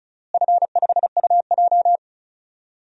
Morse code, heard as a keyed tone, about 36 wpm, 690 Hz. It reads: F5UJ
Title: DXCC-100 Call Signs - 4 Characters - Repeated 2x - 35wpm